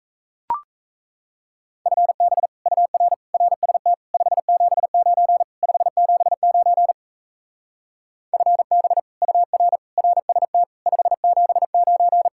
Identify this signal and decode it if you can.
Morse code, heard as a keyed tone, about 37 wpm, 700 Hz. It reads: FB UR RST 579 579 FB UR RST 579